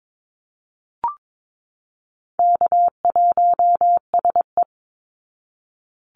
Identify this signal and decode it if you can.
Morse code, heard as a keyed tone, about 22 wpm, 700 Hz. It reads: K1SE
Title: Easy Call Signs Repeated 2x - 22wpm